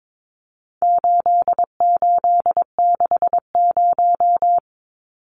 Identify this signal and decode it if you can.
Morse code, heard as a keyed tone, about 22 wpm, 700 Hz. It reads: 8860
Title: Numbers - 4-Digit 22wpm